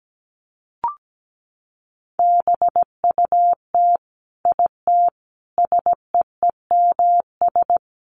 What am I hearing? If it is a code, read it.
Morse code, heard as a keyed tone, about 17 wpm, 700 Hz. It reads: BUT IT SEEMS
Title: Sets of 5 Words Encore2 17wpm